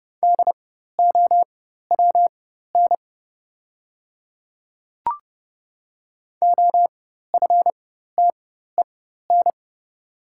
Morse code, heard as a keyed tone, about 29 wpm, spaced as for 14 wpm, 690 Hz. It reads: DOWN OFTEN
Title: 4x Character Spacing - Top 500 Words - 30wpm